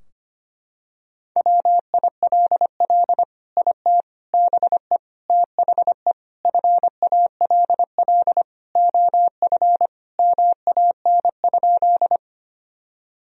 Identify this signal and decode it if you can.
Morse code, heard as a keyed tone, about 25 wpm, 700 Hz. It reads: WILL IT BE THE FALL OF MAN?
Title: Sentences from Top 1000 Words - Review Part 2 - 25wpm